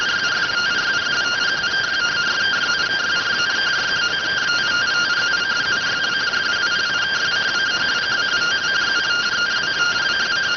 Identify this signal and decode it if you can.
Two-tone teleprinter signal, about 170 Hz shift.